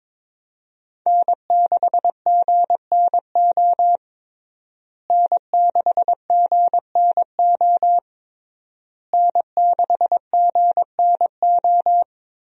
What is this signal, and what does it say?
Morse code, heard as a keyed tone, about 22 wpm, 700 Hz. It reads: N6GNO N6GNO N6GNO
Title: USA Call Signs - 5 Characters - Repeated 3x - 22wpm